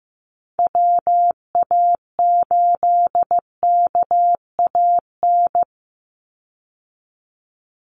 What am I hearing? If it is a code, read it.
Morse code, heard as a keyed tone, about 15 wpm, 700 Hz. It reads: WA8KAN
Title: Call Signs Repeated 3x - 15wpm